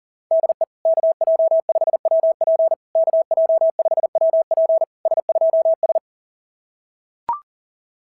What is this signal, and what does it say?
Morse code, heard as a keyed tone, about 37 wpm, 660 Hz. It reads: DE KJ5WP KJ5WP S2S
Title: SOTA QSO Practice 40wpm